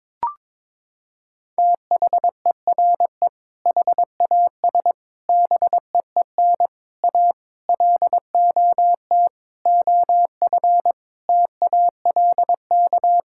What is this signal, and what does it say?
Morse code, heard as a keyed tone, about 22 wpm, 700 Hz. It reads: THERE HAS BEEN A LOT OF TALK
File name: Sentences with 7-Word Phrases - 22wpm